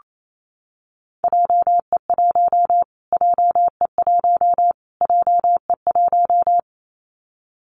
Morse code, heard as a keyed tone, about 28 wpm, 700 Hz. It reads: JE1 JE1 JE1
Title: Callsign Prefixes for Top 25 DXCC - Repeated 3x - 28wpm